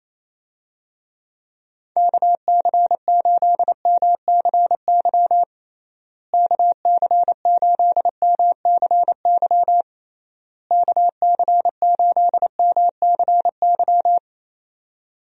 Morse code, heard as a keyed tone, about 28 wpm, 700 Hz. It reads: KC8MCY KC8MCY KC8MCY
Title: USA Call Signs - 6 Characters - Repeated 3x - 28wpm